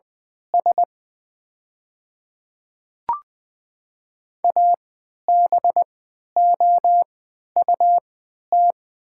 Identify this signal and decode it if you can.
Morse code, heard as a keyed tone, about 20 wpm, spaced as for 11 wpm, 700 Hz. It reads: S ABOUT
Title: 3x Character Spacing - Top 200 Words - 20wpm